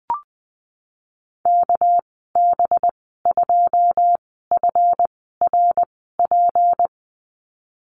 Morse code, heard as a keyed tone, about 19 wpm, 700 Hz. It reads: K B 2 F R P
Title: 238 Lesson 38g - Letter q - Call Signs 20wpm